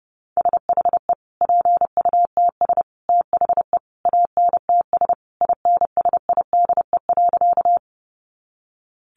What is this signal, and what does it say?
Morse code, heard as a keyed tone, about 30 wpm, 700 Hz. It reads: SHE PUTS THE ANTS INSIDE.
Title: Easy Phrases 30wpm